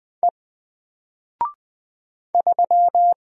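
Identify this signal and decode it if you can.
Morse code, heard as a keyed tone, about 20 wpm, 700 Hz. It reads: E 3
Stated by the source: Single letter and number Morse drill at 20 wpm, from N4LCD.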